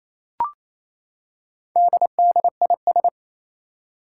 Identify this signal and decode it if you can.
Morse code, heard as a keyed tone, about 28 wpm, 700 Hz. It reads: DDIS
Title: Four Challenging Letter-Number Combinations 28wpm